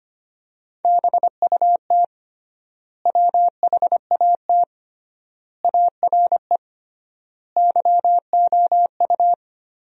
Morse code, heard as a keyed tone, about 25 wpm, 700 Hz. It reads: BUT WHAT ARE YOU
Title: 3x Word-Spacing - Sentences from Top 100 Words - 25wpm